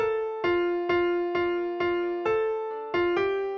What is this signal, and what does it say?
There is no code.